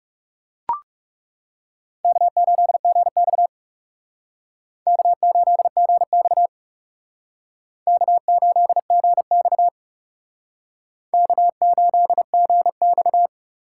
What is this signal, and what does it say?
Morse code, heard as a keyed tone, about 37 wpm, 700 Hz. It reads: K8GX K8GX K8GX K8GX
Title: Speed Racing - USA 1x2 Call Signs 30wpm